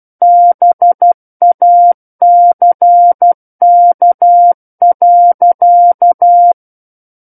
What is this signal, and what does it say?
Morse code, heard as a keyed tone, about 12 wpm, 700 Hz. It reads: BACK.
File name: Book One / 03-Chapter 3 - 12 WPM